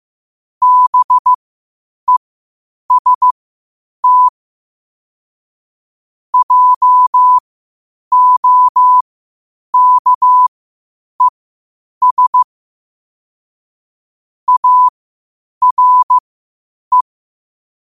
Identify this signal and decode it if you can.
Morse code, heard as a keyed tone, about 15 wpm, spaced as for 8 wpm, 1000 Hz. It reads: BEST JOKES ARE
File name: qotdfiles / 8wpm